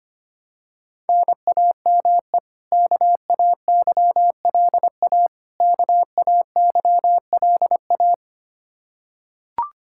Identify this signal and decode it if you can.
Morse code, heard as a keyed tone, about 25 wpm, 700 Hz. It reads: NAME KAYLA KAYLA
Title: Practice QSO - Exchange 1 - Beginning - 25wpm